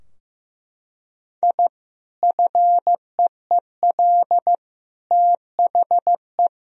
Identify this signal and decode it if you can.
Morse code, heard as a keyed tone, about 15 wpm, 700 Hz. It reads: I FEEL THE
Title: Sentences from Top 1000 Words - Review Part 3 - 15wpm